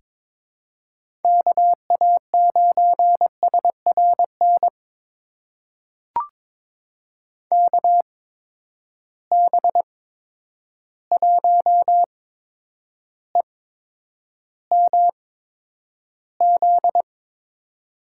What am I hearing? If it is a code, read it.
Morse code, heard as a keyed tone, about 21 wpm, spaced as for 9 wpm, 700 Hz. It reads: KA9SRN KB1EMZ
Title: USA Call Signs - 6 Characters - 8x Character Spacing - 22wpm